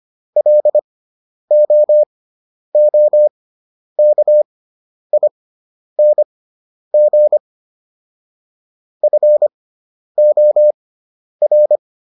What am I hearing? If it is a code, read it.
Morse code, heard as a keyed tone, about 25 wpm, spaced as for 10 wpm, 600 Hz. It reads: LOOKING FOR